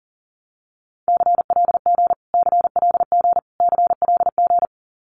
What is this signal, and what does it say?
Morse code, heard as a keyed tone, about 40 wpm, 700 Hz. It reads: CLG CLG CLG
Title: Abbreviations Repeated 3x - 40wpm